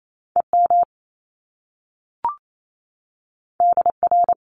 Morse code, heard as a keyed tone, about 28 wpm, 700 Hz. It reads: EM DR
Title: Top Two Letter Combinations in Words 28wpm